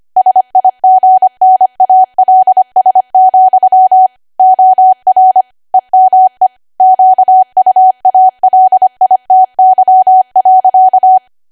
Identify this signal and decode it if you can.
Morse code, heard as a keyed tone, about 25 wpm, 750 Hz. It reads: SIGNALS, OR EME QUALITY.
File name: cwa / W1AW-Level3 / 25WPM